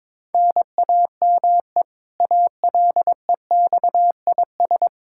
Morse code, heard as a keyed tone, about 22 wpm, 700 Hz. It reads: NAME ALEXIS
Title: K1USN SST QSO Practice 22wpm